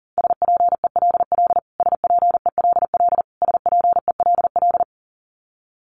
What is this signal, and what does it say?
Morse code, heard as a keyed tone, about 40 wpm, 700 Hz. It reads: SPELL SPELL SPELL
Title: Top 200 Words Repeated 40wpm